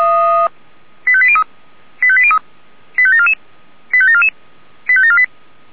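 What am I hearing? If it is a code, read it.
Five-tone selective call (ZVEI), heard as quick runs of five tones, tones of about 70 ms. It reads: 85803 85803 85744 85744 85748
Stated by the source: ZVEI five-tone call 85729 received by SDR.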